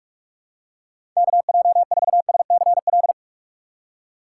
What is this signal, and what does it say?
Morse code, heard as a keyed tone, about 46 wpm, 700 Hz. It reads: KJ4SCL
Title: USA Call Signs - 6 Characters - 2x Character Spacing - 45wpm